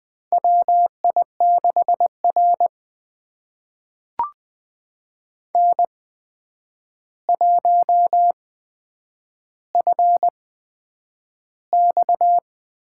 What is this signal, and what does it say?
Morse code, heard as a keyed tone, about 20 wpm, spaced as for 9 wpm, 700 Hz. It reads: WI6R N1FX
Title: USA Call Signs - 4 Characters - 8x Character Spacing - 20wpm